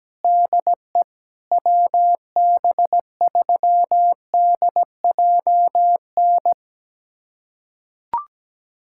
Morse code, heard as a keyed tone, about 17 wpm, 700 Hz. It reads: DE WB3DJN
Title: USA Callsign DE Callsign 17wpm